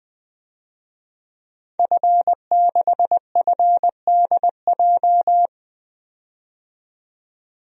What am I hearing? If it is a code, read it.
Morse code, heard as a keyed tone, about 21 wpm, 700 Hz. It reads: F6FDJ
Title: DXCC-100 Call Signs - 5 Characters - 20wpm